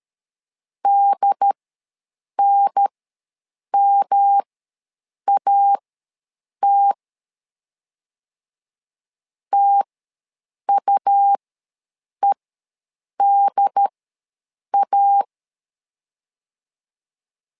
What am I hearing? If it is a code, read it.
Morse code, heard as a keyed tone, about 13 wpm, spaced as for 7 wpm, 800 Hz. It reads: DNMAT TUEDA
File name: LEZIONI / 3 MWRO / 3 M ws 12wpm 1745